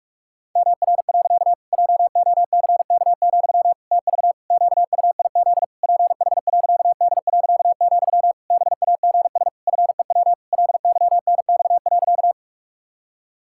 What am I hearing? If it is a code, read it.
Morse code, heard as a keyed tone, about 46 wpm, 700 Hz. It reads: MR. JOCK, TV QUIZ PH.D., BAGS FEW LYNX.
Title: Pangram Challenge v2 - 45wpm